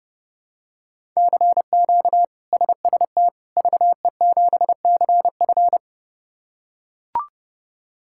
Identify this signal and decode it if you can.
Morse code, heard as a keyed tone, about 32 wpm, 700 Hz. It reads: CQ SST VE7CF
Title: K1USN SST QSO Practice - 30wpm